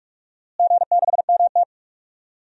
Morse code, heard as a keyed tone, about 47 wpm, 700 Hz. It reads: G6MT
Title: DXCC-100 Call Signs - 4 Characters - Repeated 3x - 45wpm